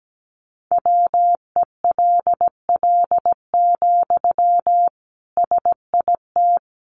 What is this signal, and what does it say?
Morse code, heard as a keyed tone, about 17 wpm, 700 Hz. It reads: WELL, SIT
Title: Sentences from Top 500 Words 17wpm